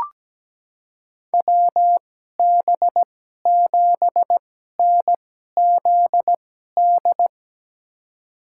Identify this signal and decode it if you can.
Morse code, heard as a keyed tone, about 17 wpm, spaced as for 12 wpm, 700 Hz. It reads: WB7NZD
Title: USA Call Signs - 6 Characters - 2x Character Spacing - 17wpm